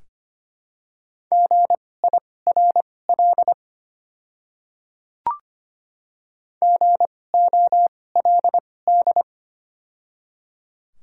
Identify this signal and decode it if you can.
Morse code, heard as a keyed tone, about 25 wpm, spaced as for 18 wpm, 700 Hz. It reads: GIRL GOLD
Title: 2x Character Spacing - Top 500 Words - 25wpm